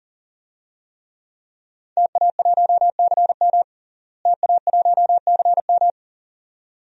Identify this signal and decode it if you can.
Morse code, heard as a keyed tone, about 40 wpm, 700 Hz. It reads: TA1CM TA1CM
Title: DXCC-100 Call Signs - 5 Characters - Repeated 2x - 40wpm